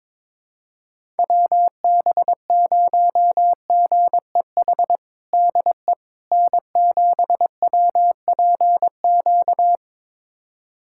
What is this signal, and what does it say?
Morse code, heard as a keyed tone, about 22 wpm, 700 Hz. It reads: WB0GEH DE N7WPQ
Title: USA Callsign DE Callsign 22wpm